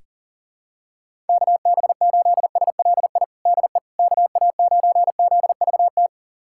Morse code, heard as a keyed tone, about 40 wpm, 700 Hz. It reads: KB8SLI DE KA9ZVT
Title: USA Callsign DE Callsign - 6 Characters - 40wpm